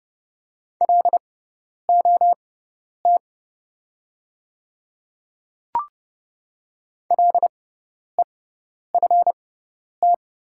Morse code, heard as a keyed tone, about 30 wpm, spaced as for 10 wpm, 700 Hz. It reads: LOT LEFT
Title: 6x Character Spacing - Top 500 Words - 30wpm